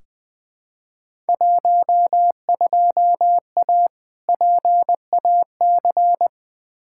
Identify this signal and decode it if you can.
Morse code, heard as a keyed tone, about 20 wpm, 700 Hz. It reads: 12A PAC
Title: Field Day Exchanges Encore - 20wpm